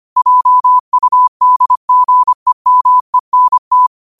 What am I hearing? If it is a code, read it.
Morse code, heard as a keyed tone, about 25 wpm, 1000 Hz. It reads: JUDGEMENT